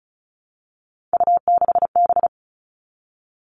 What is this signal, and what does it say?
Morse code, heard as a keyed tone, about 34 wpm, 700 Hz. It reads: U6B
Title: All Cut Numbers - 3-Digit 35wpm